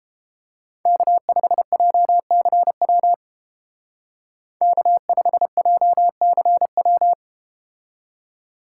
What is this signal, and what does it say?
Morse code, heard as a keyed tone, about 31 wpm, 700 Hz. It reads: K5JCW K5JCW